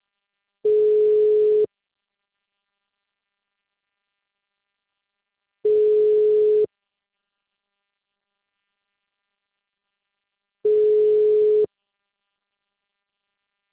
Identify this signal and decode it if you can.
Ringback tone (Europe, 425 Hz).